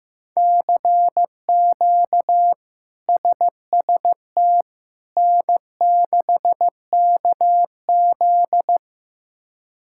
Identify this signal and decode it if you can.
Morse code, heard as a keyed tone, about 15 wpm, 700 Hz. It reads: CQ SST N6KZ
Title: K1USN SST QSO Practice - 15wpm